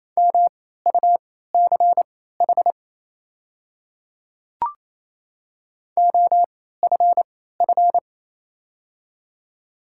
Morse code, heard as a keyed tone, about 27 wpm, spaced as for 16 wpm, 690 Hz. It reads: MUCH OFF
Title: 3x Character Spacing - Top 300 Words - 28wpm